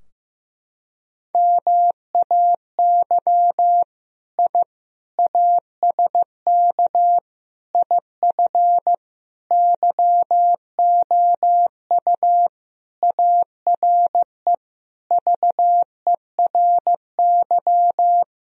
Morse code, heard as a keyed tone, about 16 wpm, 700 Hz. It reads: MAY I ASK IF YOU ARE VERY
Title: Sentences from Top 1000 Words - Review Part 3 - 15wpm